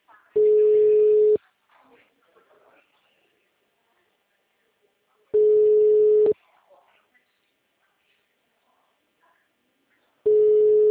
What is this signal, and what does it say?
Ringback tone (Europe, 425 Hz), with speech.